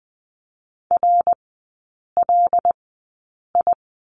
Morse code, heard as a keyed tone, about 20 wpm, spaced as for 8 wpm, 700 Hz. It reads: RLI
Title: 074 Lesson 13e - Letter L - Three Characters 20wpm